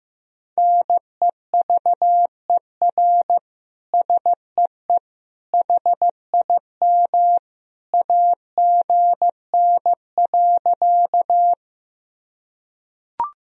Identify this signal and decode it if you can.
Morse code, heard as a keyed tone, about 15 wpm, 700 Hz. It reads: NEVER SEE HIM AGN.
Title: Sentences from Top 300 Words and Abbreviations 15wpm